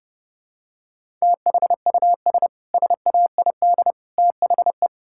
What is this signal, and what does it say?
Morse code, heard as a keyed tone, about 32 wpm, 700 Hz. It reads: THUS SAID THE
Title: Sentences from Top 1000 Words - Review Part 5 - 30wpm